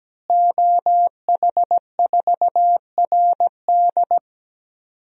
Morse code, heard as a keyed tone, about 17 wpm, 700 Hz. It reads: OH4RD
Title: DXCC-100 Call Signs - 5 Characters - Repeated 3x - 17wpm